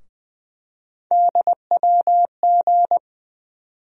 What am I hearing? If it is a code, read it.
Morse code, heard as a keyed tone, about 20 wpm, 700 Hz. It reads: DWG
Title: USA Call Sign Postfixes - 3 Letters 20wpm